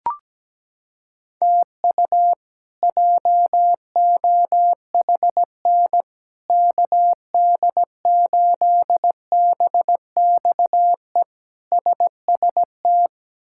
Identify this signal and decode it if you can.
Morse code, heard as a keyed tone, about 17 wpm, 700 Hz. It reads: TU JOHN KD8BXE SST
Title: K1USN SST QSO Practice - 17wpm